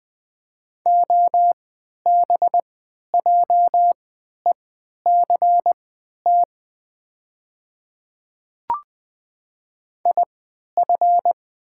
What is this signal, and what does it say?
Morse code, heard as a keyed tone, about 20 wpm, spaced as for 11 wpm, 700 Hz. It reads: OBJECT IF
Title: 3x Character Spacing - Top 500 Words - 20wpm